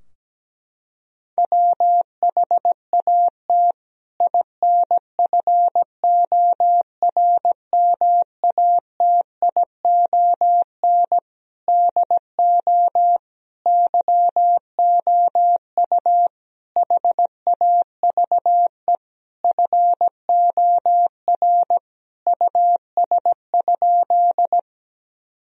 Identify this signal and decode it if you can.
Morse code, heard as a keyed tone, about 17 wpm, 700 Hz. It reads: WHAT INFORMATION DO YOU HAVE FOR US?
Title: Sentences from Top 1000 Words - Review Part 3 - 17wpm